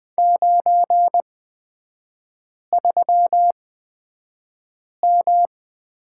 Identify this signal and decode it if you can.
Morse code, heard as a keyed tone, about 20 wpm, spaced as for 5 wpm, 700 Hz. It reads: 93M
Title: 243 Lesson 39e - Letter z - Three Characters 20wpm